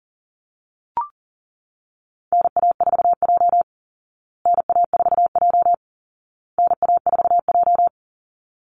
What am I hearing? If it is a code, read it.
Morse code, heard as a keyed tone, about 40 wpm, 700 Hz. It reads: NA4J NA4J NA4J
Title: Call Signs Repeated 2x - 40wpm